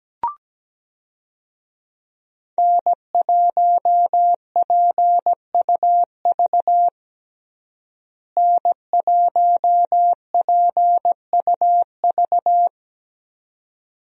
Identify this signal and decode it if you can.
Morse code, heard as a keyed tone, about 17 wpm, 700 Hz. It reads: N1PUV N1PUV
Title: USA Call Signs - 5 Characters - Repeated 2x - 17wpm